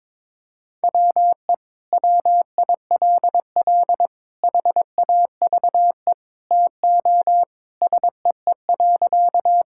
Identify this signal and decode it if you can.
Morse code, heard as a keyed tone, about 22 wpm, 700 Hz. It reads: WE WILL HAVE TO SEE.
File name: practice / 22wpm / Sentences with 3-Word Phrases - 22wpm